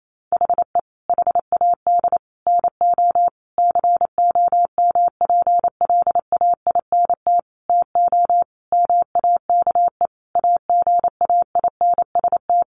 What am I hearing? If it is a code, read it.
Morse code, heard as a keyed tone, about 28 wpm, 700 Hz. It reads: HE HAD NO COMPLAINT TO MAKE AGAINST
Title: Sentences from Top 1400 Words 28wpm